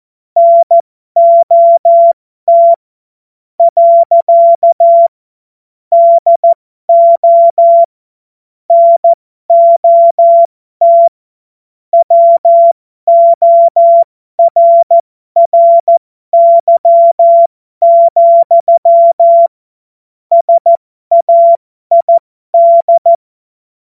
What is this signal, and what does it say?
Morse code, heard as a keyed tone, about 14 wpm, spaced as for 12 wpm, 680 Hz. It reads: NOT. DO NOT WORRY, SAID
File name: Morsekurs / Witze